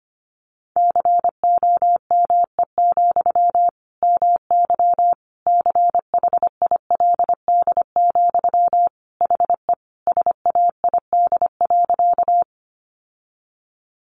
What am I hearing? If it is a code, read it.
Morse code, heard as a keyed tone, about 25 wpm, 700 Hz. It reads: COME, MY CHILD, HE SAID.
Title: Sentences from Top 800 Words 25wpm